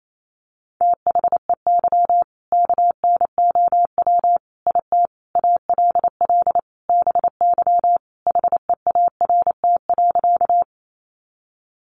Morse code, heard as a keyed tone, about 28 wpm, 700 Hz. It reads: THEY KNOW IT ALL BY HEART.
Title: Sentences from Top 1200 Words 28wpm